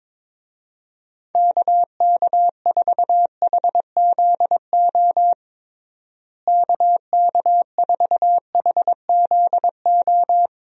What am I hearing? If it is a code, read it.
Morse code, heard as a keyed tone, about 22 wpm, 700 Hz. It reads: KK4HZO KK4HZO
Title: USA Call Signs - 6 Characters - Repeated 2x - 22wpm